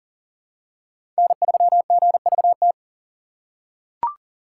Morse code, heard as a keyed tone, about 40 wpm, 700 Hz. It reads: N3GVT
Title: USA Call Signs - 5 Characters - 2x Character Spacing - 40wpm